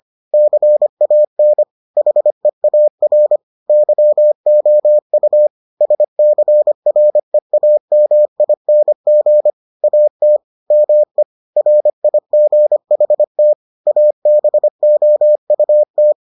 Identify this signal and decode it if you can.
Morse code, heard as a keyed tone, about 25 wpm, 600 Hz. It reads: CAN HEAR YOU SCREAMING AT ME RIGHT ABOUT